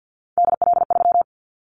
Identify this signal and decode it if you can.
Morse code, heard as a keyed tone, about 50 wpm, 700 Hz. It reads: DL3
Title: Callsign Prefixes for Top 25 DXCC - 50wpm